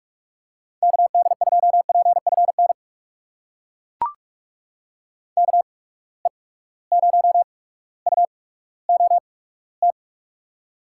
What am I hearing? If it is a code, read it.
Morse code, heard as a keyed tone, about 42 wpm, spaced as for 19 wpm, 700 Hz. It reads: KD2PFN KE0UOT
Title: USA Call Signs - 6 Characters - 8x Character Spacing - 45wpm